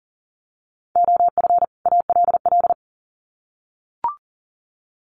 Morse code, heard as a keyed tone, about 40 wpm, 700 Hz. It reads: OF ALL
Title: Sets of 2 Words 40wpm